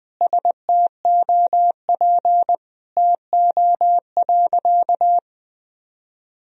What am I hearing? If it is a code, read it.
Morse code, heard as a keyed tone, about 21 wpm, 700 Hz. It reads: STOP TO.
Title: Sentences from Top 1000 Words - Review Part 1 - 20wpm